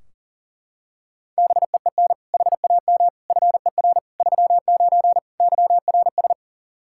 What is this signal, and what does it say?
Morse code, heard as a keyed tone, about 40 wpm, 700 Hz. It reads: BEEN HAM FER 39 YRS
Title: Practice QSO - Exchange 3 - More Info - 40wpm